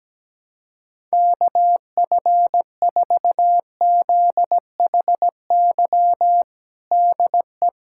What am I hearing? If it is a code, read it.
Morse code, heard as a keyed tone, about 17 wpm, 700 Hz. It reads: KF4ZHY DE
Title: USA Callsign DE Callsign 17wpm